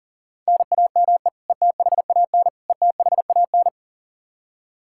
Morse code, heard as a keyed tone, about 40 wpm, 700 Hz. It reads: NAME ETHAN ETHAN
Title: Practice QSO - Exchange 1 - Beginning - 40wpm